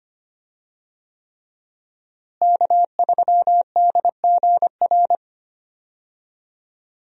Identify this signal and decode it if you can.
Morse code, heard as a keyed tone, about 25 wpm, 700 Hz. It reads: K3DGR
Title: USA Call Signs - 5 Characters - Repeated 3x - 25wpm